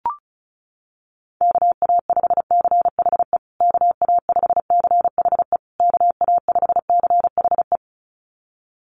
Morse code, heard as a keyed tone, about 34 wpm, 700 Hz. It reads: KA5CHE KA5CHE KA5CHE
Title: USA Call Signs Repeated 3x 35wpm